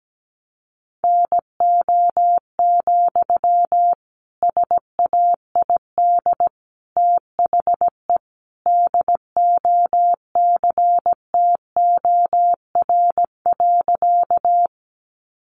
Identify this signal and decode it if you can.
Morse code, heard as a keyed tone, about 17 wpm, 700 Hz. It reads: NO, SAID THE DOCTOR.
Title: Sentences from Top 1500 Words 17wpm